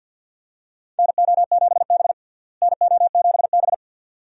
Morse code, heard as a keyed tone, about 49 wpm, 700 Hz. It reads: NO7B NO7B
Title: USA Call Signs - 4 Characters - Repeated 2x - 50wpm